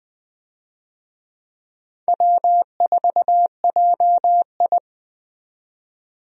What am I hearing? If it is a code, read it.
Morse code, heard as a keyed tone, about 20 wpm, 700 Hz. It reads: W4JI
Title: USA Call Signs - 4 Characters - 20wpm